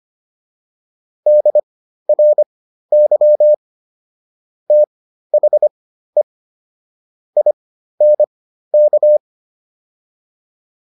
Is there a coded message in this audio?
Yes: Morse code from 1.3 to 9.2 s, heard as a keyed tone, about 25 wpm, spaced as for 13 wpm, 600 Hz. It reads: DRY THE INK